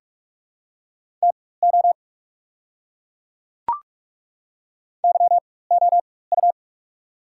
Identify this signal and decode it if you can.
Morse code, heard as a keyed tone, about 43 wpm, spaced as for 21 wpm, 690 Hz. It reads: TO YOU
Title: 4x Character Spacing - Top 500 Words - 45wpm